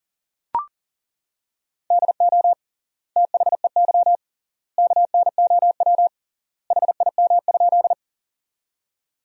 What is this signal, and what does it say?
Morse code, heard as a keyed tone, about 40 wpm, 700 Hz. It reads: DO THEY KNOW HIM?
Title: 3x Word-Spacing - Sentences from Top 100 Words - 40wpm